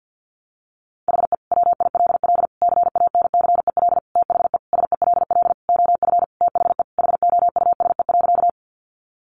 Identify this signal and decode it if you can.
Morse code, heard as a keyed tone, about 50 wpm, 700 Hz. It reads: HE WILL CANCEL THE SELL OF THE HOUSE.
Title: Sentences from Top 1200 Words 50wpm